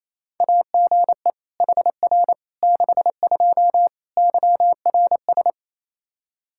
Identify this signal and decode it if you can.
Morse code, heard as a keyed tone, about 28 wpm, 700 Hz. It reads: AGE HR 62 YRS
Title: Practice QSO - Exchange 3 - More Info - 28wpm